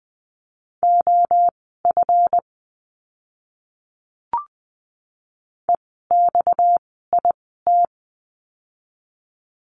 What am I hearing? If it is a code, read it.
Morse code, heard as a keyed tone, about 20 wpm, spaced as for 14 wpm, 700 Hz. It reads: OF EXIT
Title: 230 Lesson 37f - Letter x - Words 20wpm